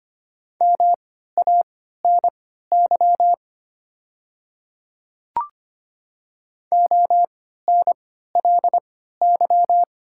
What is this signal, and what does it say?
Morse code, heard as a keyed tone, about 25 wpm, spaced as for 14 wpm, 700 Hz. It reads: MANY ONLY
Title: 3x Character Spacing - Top 200 Words - 25wpm